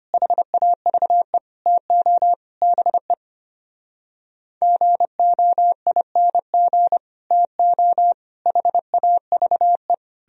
Morse code, heard as a keyed tone, about 33 wpm, 700 Hz. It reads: HAVE TO BE GOING TO HAVE